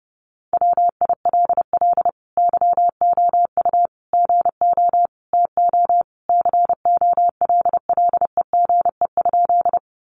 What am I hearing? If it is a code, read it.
Morse code, heard as a keyed tone, about 30 wpm, 700 Hz. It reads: WILL YOU GO TO COLLEGE?
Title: Sentences from Top 1300 Words 30wpm